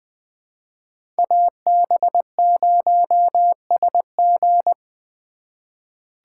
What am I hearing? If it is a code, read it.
Morse code, heard as a keyed tone, about 20 wpm, 700 Hz. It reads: AB0SG
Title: USA Call Signs - 5 Characters - 3x Character Spacing - 20wpm